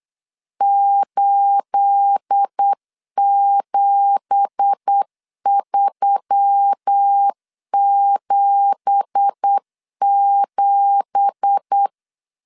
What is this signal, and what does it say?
Morse code, heard as a keyed tone, about 8 wpm, 800 Hz. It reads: 87377
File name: 8wpm